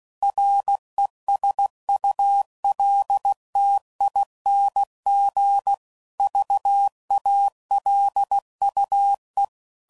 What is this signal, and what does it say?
Morse code, heard as a keyed tone, about 16 wpm, 800 Hz. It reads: RESULTING VALUE
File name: CW Code Groups @15WPM / Technical Subjects / XC